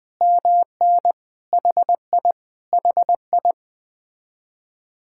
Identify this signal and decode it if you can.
Morse code, heard as a keyed tone, about 20 wpm, 700 Hz. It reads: MN HI HI
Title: Speed-Racing Common QSO Phrases Encore - 17wpm